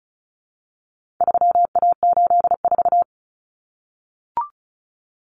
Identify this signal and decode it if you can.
Morse code, heard as a keyed tone, about 34 wpm, 700 Hz. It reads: 3A84
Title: Common Cut Numbers - 4-Digit 35wpm